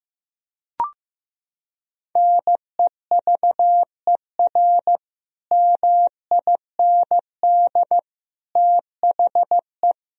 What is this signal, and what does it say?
Morse code, heard as a keyed tone, about 15 wpm, 700 Hz. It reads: NEVER MIND THE
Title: Sentences from Top 1000 Words - Review Part 1 - 15wpm